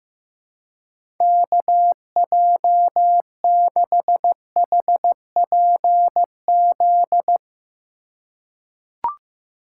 Morse code, heard as a keyed tone, about 15 wpm, 700 Hz. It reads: KJ6HPZ
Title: USA Call Signs - 6 Characters - 3x Character Spacing - 15wpm